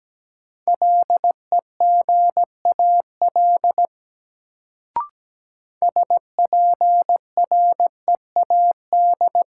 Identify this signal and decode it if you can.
Morse code, heard as a keyed tone, about 17 wpm, 700 Hz. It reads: LEGAL SPREAD
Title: Sentences from Top 1200 Words - New Word Drill - 17wpm